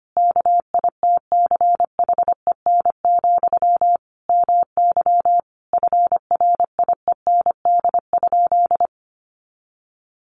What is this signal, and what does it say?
Morse code, heard as a keyed tone, about 25 wpm, 700 Hz. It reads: KITCHEN, MY FRIEND?
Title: Sentences from Top 1400 Words 25wpm